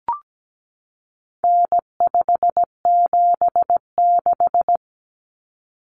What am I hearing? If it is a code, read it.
Morse code, heard as a keyed tone, about 17 wpm, 700 Hz. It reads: N576
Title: Common Cut Numbers - 4-Digit 17wpm